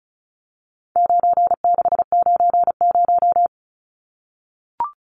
Morse code, heard as a keyed tone, about 34 wpm, 700 Hz. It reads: 9690
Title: Numbers - 4-Digit 35wpm